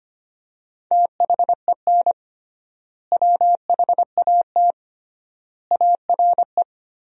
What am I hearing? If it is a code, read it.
Morse code, heard as a keyed tone, about 25 wpm, 700 Hz. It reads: THEN WHAT ARE
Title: 3x Word-Spacing - Sentences from Top 100 Words - 25wpm